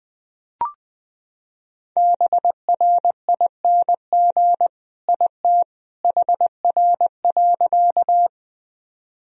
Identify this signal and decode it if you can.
Morse code, heard as a keyed tone, about 20 wpm, 700 Hz. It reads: BRING IT HR.
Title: Sentences from Top 500 Words and Abbreviations 20wpm